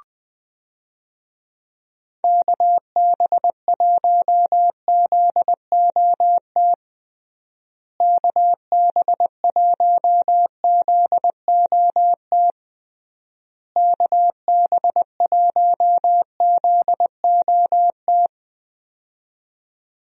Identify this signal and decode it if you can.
Morse code, heard as a keyed tone, about 20 wpm, 700 Hz. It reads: KB1ZOT KB1ZOT KB1ZOT
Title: USA Call Signs - 6 Characters - Repeated 3x - 20wpm